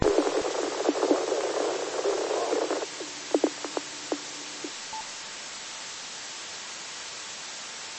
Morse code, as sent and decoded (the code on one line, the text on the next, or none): none